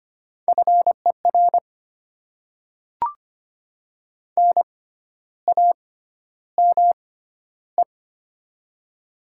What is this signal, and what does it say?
Morse code, heard as a keyed tone, about 25 wpm, spaced as for 12 wpm, 700 Hz. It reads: FER NAME
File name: Top 20 QSO Elements - 6x Character Spacing - 25wpm